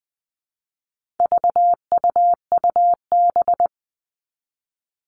Morse code, heard as a keyed tone, about 20 wpm, 700 Hz. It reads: VUUB
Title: All Cut Numbers - 4-Digit 20wpm